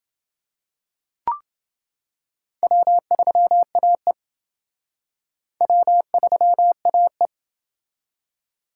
Morse code, heard as a keyed tone, about 28 wpm, 700 Hz. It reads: W3AE W3AE